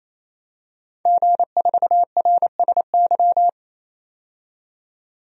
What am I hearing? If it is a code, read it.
Morse code, heard as a keyed tone, about 27 wpm, 690 Hz. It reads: G4RSY
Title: DXCC-100 Call Signs - 5 Characters - 28wpm